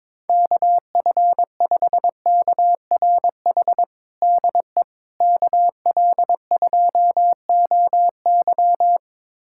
Morse code, heard as a keyed tone, about 22 wpm, 700 Hz. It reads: KF5KRH DE KL2OY
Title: USA Callsign DE Callsign 22wpm